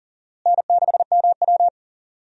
Morse code, heard as a keyed tone, about 40 wpm, 700 Hz. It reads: N6MW
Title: USA Call Signs - 4 Characters - Repeated 3x - 40wpm